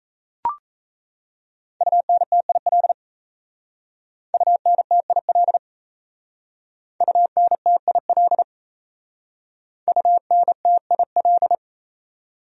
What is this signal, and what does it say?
Morse code, heard as a keyed tone, about 38 wpm, 700 Hz. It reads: UNTIL UNTIL UNTIL UNTIL